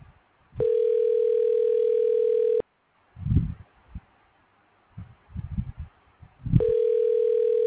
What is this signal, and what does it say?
Ringback tone (North America), 440 + 480 Hz.